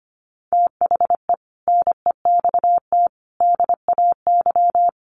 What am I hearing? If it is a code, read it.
Morse code, heard as a keyed tone, about 25 wpm, 700 Hz. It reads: THE NEXT DAY
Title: Sets of 3 Words 25wpm